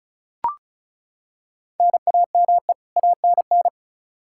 Morse code, heard as a keyed tone, about 34 wpm, 700 Hz. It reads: NAME ANN
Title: K1USN SST QSO Practice 35wpm